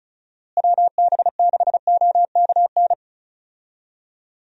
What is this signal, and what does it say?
Morse code, heard as a keyed tone, about 36 wpm, 700 Hz. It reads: WB6OKN
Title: USA Call Signs - 6 Characters - Repeated 3x - 35wpm